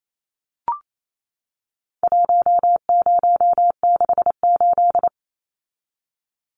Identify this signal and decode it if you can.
Morse code, heard as a keyed tone, about 28 wpm, 700 Hz. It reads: 1068
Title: Numbers - 4-Digit 28wpm